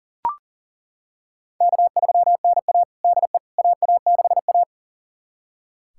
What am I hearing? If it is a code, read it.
Morse code, heard as a keyed tone, about 40 wpm, 700 Hz. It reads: K3NA DE AA6A
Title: USA Callsign DE Callsign - 4 Characters - 40wpm